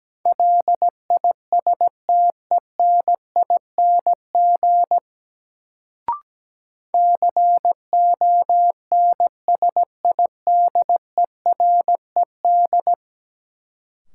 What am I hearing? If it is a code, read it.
Morse code, heard as a keyed tone, about 17 wpm, 700 Hz. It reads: LISTENING CONSIDERED
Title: Sets of 5 Words - New Word Drill - 17wpm